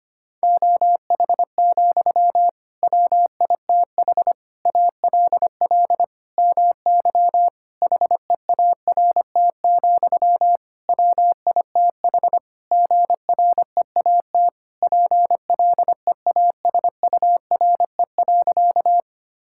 Morse code, heard as a keyed tone, about 25 wpm, 700 Hz. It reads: OH, WITH ALL MY HEART, WITH GREAT PLEASURE.
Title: Sentences from Top 1600 Words 25wpm